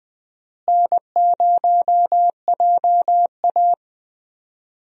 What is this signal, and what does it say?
Morse code, heard as a keyed tone, about 20 wpm, 700 Hz. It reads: N0JA
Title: USA Call Signs - 4 Characters - Repeated 2x - 20wpm